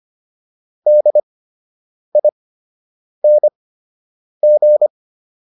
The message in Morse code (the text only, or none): DING